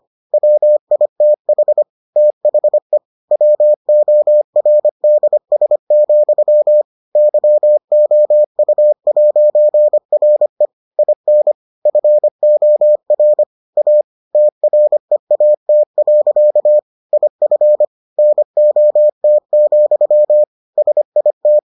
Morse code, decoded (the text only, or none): WITH THE WORDS, YOU'RE IN FOR A TREAT. IF NOT, SIT